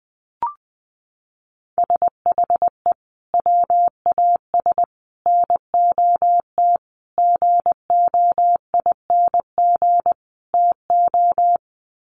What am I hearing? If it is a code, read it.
Morse code, heard as a keyed tone, about 20 wpm, 700 Hz. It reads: SHE WAS NOT GOING TO
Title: Sets of 5 Words Encore2 20wpm